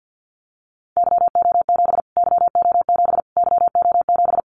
Morse code, heard as a keyed tone, about 50 wpm, 700 Hz. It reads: YO7 YO7 YO7
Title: Callsign Prefixes for Top 25 DXCC - Repeated 3x - 50wpm